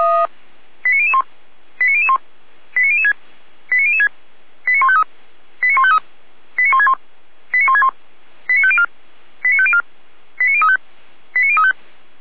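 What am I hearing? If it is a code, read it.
Five-tone selective call (ZVEI), heard as quick runs of five tones, tones of about 70 ms. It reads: 89001 89001 89006 89006 89153 89153 89161 89161 89504 89504 89035 89035
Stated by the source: ZVEI five-tone selective call 89160, received on SDR.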